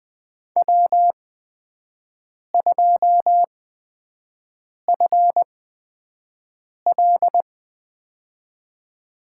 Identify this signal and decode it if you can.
Morse code, heard as a keyed tone, about 20 wpm, spaced as for 5 wpm, 700 Hz. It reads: W2FL
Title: USA Call Signs - 4 Characters - 8x Character Spacing - 20wpm